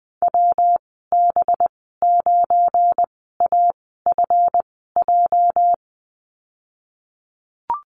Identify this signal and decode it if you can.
Morse code, heard as a keyed tone, about 20 wpm, spaced as for 15 wpm, 700 Hz. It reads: WB9AFJ
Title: 219 Lesson 35g - Number 0 - Call Signs 20wpm